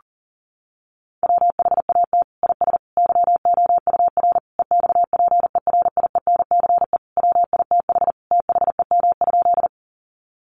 Morse code, heard as a keyed tone, about 37 wpm, 700 Hz. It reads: WHAT IS YOUR EXPERIENCE WITH THEM?
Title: Sentences from Top 600 Words 40wpm